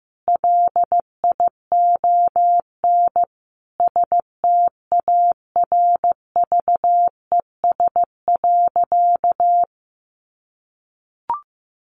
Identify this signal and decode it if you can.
Morse code, heard as a keyed tone, about 15 wpm, 700 Hz. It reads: LION STARVES.
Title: A Sound Of Thunder 15wpm